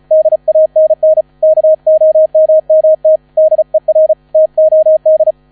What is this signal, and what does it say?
Morse code, heard as a keyed tone, about 34 wpm, 630 Hz. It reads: DANN KOMMT DER TOD
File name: 35wpm